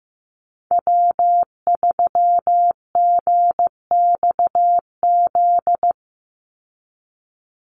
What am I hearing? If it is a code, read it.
Morse code, heard as a keyed tone, about 15 wpm, 700 Hz. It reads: W3GXZ
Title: USA Call Signs 15wpm